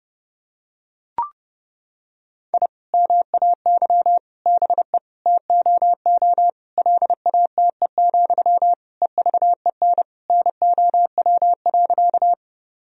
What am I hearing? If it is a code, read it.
Morse code, heard as a keyed tone, about 30 wpm, 700 Hz. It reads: I MAY BE TOO LATE, EVEN NOW.
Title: Sentences from Top 1000 Words - Review Part 1 - 30wpm